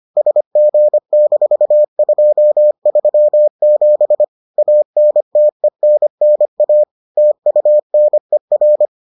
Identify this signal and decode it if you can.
Morse code, heard as a keyed tone, about 25 wpm, 600 Hz. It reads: SG-237 ANTENNA TUNER